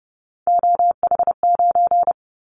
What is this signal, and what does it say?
Morse code, heard as a keyed tone, about 30 wpm, 700 Hz. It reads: OH9
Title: Callsign Prefixes for Top 50 DXCC - 30wpm